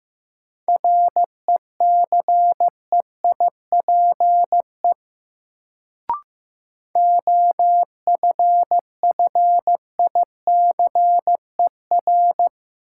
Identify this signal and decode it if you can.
Morse code, heard as a keyed tone, about 15 wpm, 700 Hz. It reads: RECEIPE OFFICER
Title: Sentences from Top 1200 Words - New Word Drill - 15wpm